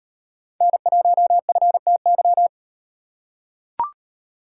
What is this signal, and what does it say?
Morse code, heard as a keyed tone, about 38 wpm, 700 Hz. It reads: N1FTY